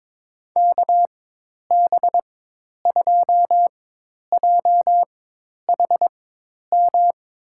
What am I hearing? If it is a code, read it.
Morse code, heard as a keyed tone, about 22 wpm, spaced as for 10 wpm, 700 Hz. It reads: KB2JHM
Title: USA Call Signs - 6 Characters - 4x Character Spacing - 22wpm